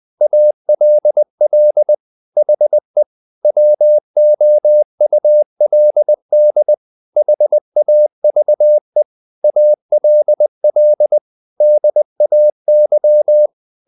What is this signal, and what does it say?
Morse code, heard as a keyed tone, about 20 wpm, 600 Hz. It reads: ALL HE WOULD HAVE ALL DAY
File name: the old man and the sea / the old man and the sea part 1 20wpm